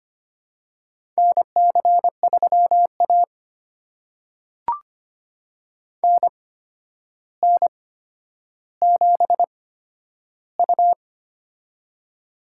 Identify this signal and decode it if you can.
Morse code, heard as a keyed tone, about 25 wpm, spaced as for 11 wpm, 700 Hz. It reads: NC3A NN7U
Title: USA Call Signs - 4 Characters - 8x Character Spacing - 25wpm